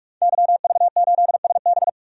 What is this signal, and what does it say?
Morse code, heard as a keyed tone, about 46 wpm, 700 Hz. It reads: YV8SB
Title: DXCC-100 Call Signs - 5 Characters - 45wpm